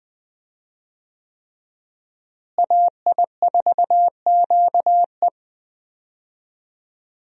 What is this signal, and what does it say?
Morse code, heard as a keyed tone, about 20 wpm, 700 Hz. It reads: AI4QE
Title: USA Call Signs - 5 Characters - Repeated 3x - 20wpm